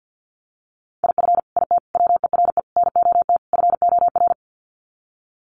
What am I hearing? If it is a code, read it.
Morse code, heard as a keyed tone, about 48 wpm, 700 Hz. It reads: IF IT WERE NOT FOR
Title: Sets of 5 Words 50wpm